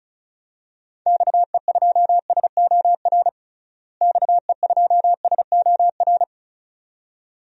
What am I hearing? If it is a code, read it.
Morse code, heard as a keyed tone, about 36 wpm, 700 Hz. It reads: XE2SOR XE2SOR
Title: DXCC-100 Call Signs - 6 Characters - Repeated 2x - 35wpm